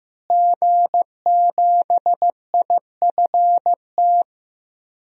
Morse code, heard as a keyed tone, about 15 wpm, 700 Hz. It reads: G7IFT
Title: DXCC-100 Call Signs - 5 Characters - Repeated 3x - 15wpm